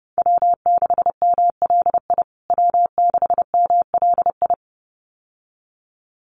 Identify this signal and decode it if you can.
Morse code, heard as a keyed tone, about 28 wpm, 700 Hz. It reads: W6MLI W6MLI
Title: USA Call Signs Repeated 2x 30wpm